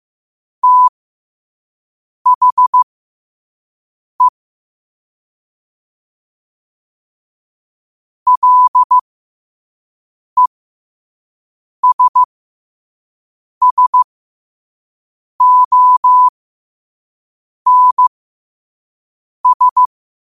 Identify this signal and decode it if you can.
Morse code, heard as a keyed tone, about 15 wpm, spaced as for 5 wpm, 1000 Hz. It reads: THE LESSONS
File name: qotdfiles / 5wpm